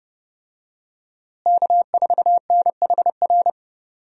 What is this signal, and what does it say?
Morse code, heard as a keyed tone, about 30 wpm, 700 Hz. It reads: K4NHR
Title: USA Call Signs - 5 Characters - Repeated 3x - 30wpm